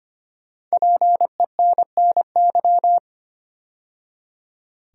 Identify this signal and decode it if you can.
Morse code, heard as a keyed tone, about 25 wpm, 700 Hz. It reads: PENNY
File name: 5 Letter Words - First 3 Letters Given - 25wpm